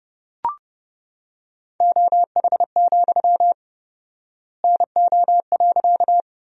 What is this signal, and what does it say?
Morse code, heard as a keyed tone, about 30 wpm, 700 Hz. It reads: OH, NO.
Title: 4x Word-Spacing - Sentences from Top 500 Words - 30wpm